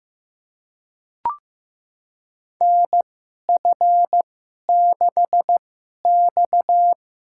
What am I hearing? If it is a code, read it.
Morse code, heard as a keyed tone, about 15 wpm, spaced as for 11 wpm, 700 Hz. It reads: NF6X
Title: USA Call Signs - 2x Character Spacing - 15wpm